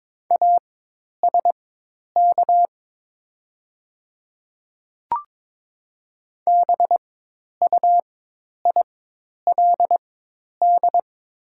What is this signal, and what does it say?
Morse code, heard as a keyed tone, about 22 wpm, spaced as for 10 wpm, 700 Hz. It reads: ASK BUILD
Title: 4x Character Spacing - Top 200 Words - 22wpm